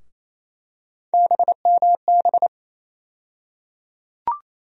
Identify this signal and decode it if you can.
Morse code, heard as a keyed tone, about 28 wpm, 700 Hz. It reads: BMB
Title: USA Call Sign Postfixes - 3 Letters 28wpm